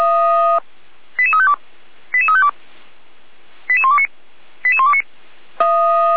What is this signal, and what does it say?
Five-tone selective call (ZVEI), heard as quick runs of five tones, tones of about 80 ms. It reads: 88352 88352 88129 88129